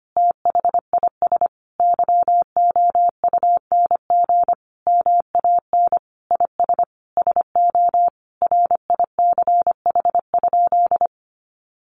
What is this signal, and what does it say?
Morse code, heard as a keyed tone, about 25 wpm, 700 Hz. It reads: THIS YOUNG MAN IS SO RICH?
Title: Sentences from Top 1100 Words 25wpm